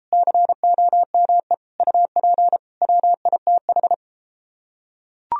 Morse code, heard as a keyed tone, about 33 wpm, 700 Hz. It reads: COME UP WITH